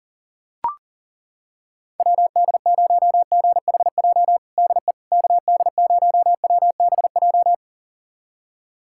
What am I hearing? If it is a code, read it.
Morse code, heard as a keyed tone, about 40 wpm, 700 Hz. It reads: WD0GHJ DE KD0WBJ
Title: USA Callsign DE Callsign - 6 Characters - 40wpm